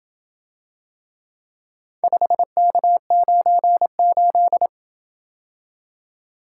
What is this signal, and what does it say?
Morse code, heard as a keyed tone, about 27 wpm, 700 Hz. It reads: 5K98